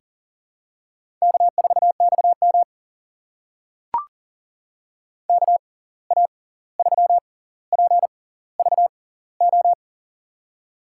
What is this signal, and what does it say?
Morse code, heard as a keyed tone, about 40 wpm, spaced as for 18 wpm, 700 Hz. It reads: K4XM KA3PVO
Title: USA Call Signs - 6x Character Spacing - 40wpm